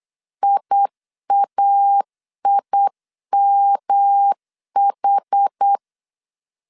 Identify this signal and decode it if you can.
Morse code, heard as a keyed tone, about 8 wpm, 800 Hz. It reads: IAIMH
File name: CW / DET / 1 3 1500C S B 8wpm 4141